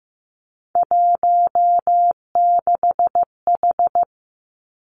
Morse code, heard as a keyed tone, about 15 wpm, 700 Hz. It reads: 16H
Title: Three Challenging Letter-Number Combinations 15wpm